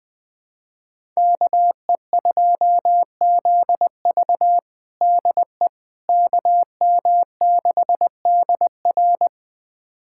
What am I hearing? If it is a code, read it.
Morse code, heard as a keyed tone, about 20 wpm, 700 Hz. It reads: KE2ZV DE KM6DR
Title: USA Callsign DE Callsign - 5 Characters - 20wpm